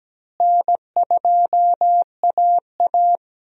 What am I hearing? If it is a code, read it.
Morse code, heard as a keyed tone, about 17 wpm, 700 Hz. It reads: N2AA
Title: USA Call Signs - 4 Characters - Repeated 3x - 17wpm